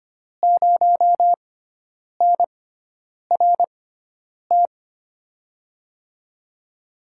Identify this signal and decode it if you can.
Morse code, heard as a keyed tone, about 25 wpm, spaced as for 9 wpm, 700 Hz. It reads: 0NRT
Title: USA Call Signs - 6x Character Spacing - 25wpm